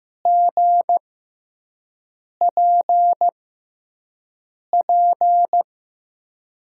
Morse code, heard as a keyed tone, about 15 wpm, spaced as for 5 wpm, 700 Hz. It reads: GPP
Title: USA Call Signs - 5 Characters - 6x Character Spacing - 15wpm